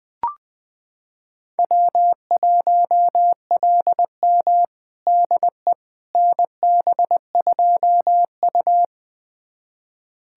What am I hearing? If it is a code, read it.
Morse code, heard as a keyed tone, about 21 wpm, 700 Hz. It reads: W1LM DE NB2U
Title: USA Callsign DE Callsign - 4 Characters - 20wpm